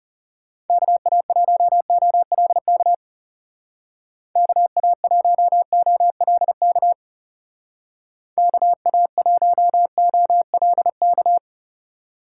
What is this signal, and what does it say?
Morse code, heard as a keyed tone, about 36 wpm, 700 Hz. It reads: KA1OLK KA1OLK KA1OLK
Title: Speed Racing - USA 2x3 Call Signs 30wpm